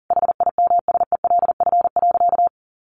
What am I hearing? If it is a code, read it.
Morse code, heard as a keyed tone, about 40 wpm, 700 Hz. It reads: HIMSELF.
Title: A Sound Of Thunder 40wpm